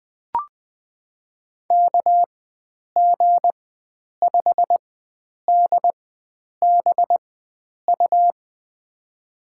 Morse code, heard as a keyed tone, about 20 wpm, spaced as for 9 wpm, 700 Hz. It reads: KG5DBU
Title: USA Call Signs - 6 Characters - 4x Character Spacing - 20wpm